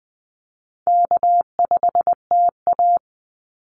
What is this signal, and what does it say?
Morse code, heard as a keyed tone, about 20 wpm, 700 Hz. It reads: K5TA
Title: Call Signs Repeated 2x - 20wpm